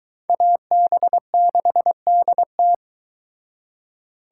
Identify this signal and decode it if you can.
Morse code, heard as a keyed tone, about 22 wpm, 700 Hz. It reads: AB6DT